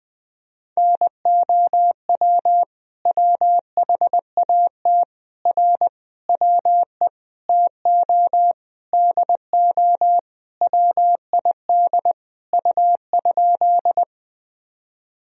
Morse code, heard as a keyed tone, about 20 wpm, 700 Hz. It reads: NOW WHAT R WE TO DO WID U?
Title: Sentences from Top 100 Words and Abbreviations 20wpm